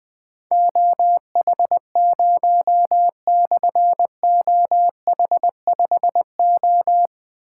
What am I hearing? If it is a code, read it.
Morse code, heard as a keyed tone, about 20 wpm, 700 Hz. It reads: OH0/OH5O
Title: Difficult Call Signs 20wpm